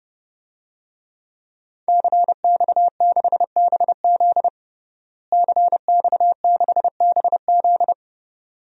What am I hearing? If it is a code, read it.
Morse code, heard as a keyed tone, about 30 wpm, 700 Hz. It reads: CX6BZ CX6BZ
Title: DXCC-100 Call Signs - 5 Characters - Repeated 2x - 30wpm